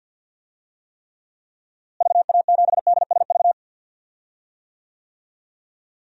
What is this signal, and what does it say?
Morse code, heard as a keyed tone, about 47 wpm, 700 Hz. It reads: VA7DSV
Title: DXCC-100 Call Signs - 6 Characters - Repeated 3x - 50wpm